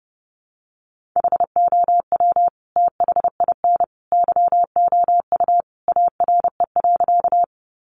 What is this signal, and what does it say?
Morse code, heard as a keyed tone, about 30 wpm, 700 Hz. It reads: HOW THIN YOU ARE.
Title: Sentences from Top 1200 Words 30wpm